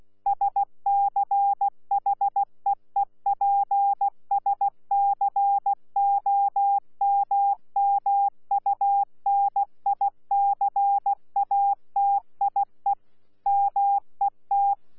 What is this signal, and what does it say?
Morse code, heard as a keyed tone, about 16 wpm, 800 Hz. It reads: SCHEEPSCOMMUNICATIE MET
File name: Z PI4VRZ / PI4VRZ 14-jun-2025